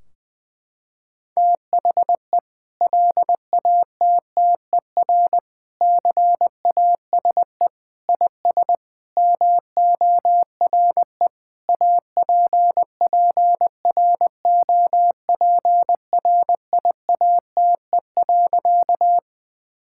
Morse code, heard as a keyed tone, about 20 wpm, 700 Hz. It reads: THE LATTER CASE IS MORE APPROPRIATE.
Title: Sentences from Top 1600 Words 20wpm